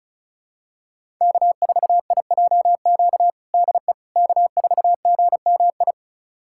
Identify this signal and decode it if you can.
Morse code, heard as a keyed tone, about 36 wpm, 700 Hz. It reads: K4IJQ DE K4GMI
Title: USA Callsign DE Callsign 35wpm